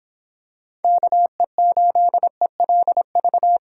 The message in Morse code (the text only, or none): KE8ELV